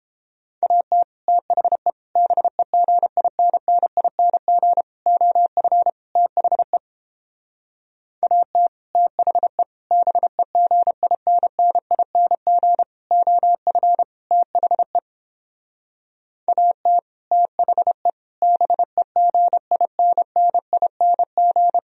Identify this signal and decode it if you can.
Morse code, heard as a keyed tone, about 31 wpm, 700 Hz. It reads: AT THE BEGINNING OF THE AT THE BEGINNING OF THE AT THE BEGINNING